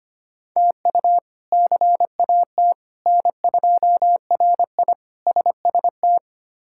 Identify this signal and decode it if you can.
Morse code, heard as a keyed tone, about 25 wpm, 700 Hz. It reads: TU CAT N2RI SST
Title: K1USN SST QSO Practice - 25wpm